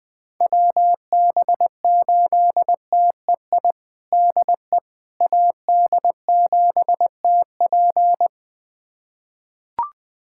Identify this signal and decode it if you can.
Morse code, heard as a keyed tone, about 19 wpm, 700 Hz. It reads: WB8TEI DE AD7TP
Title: USA Callsign DE Callsign 20wpm